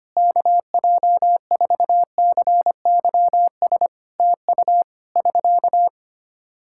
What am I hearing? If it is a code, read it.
Morse code, heard as a keyed tone, about 25 wpm, 700 Hz. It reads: KJ4CYS TU <SK>
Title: Practice QSO - Exchange 4 - Ending - 25wpm